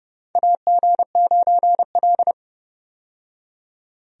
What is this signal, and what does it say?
Morse code, heard as a keyed tone, about 31 wpm, 700 Hz. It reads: AG9L
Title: USA Call Signs - 4 Characters - 30wpm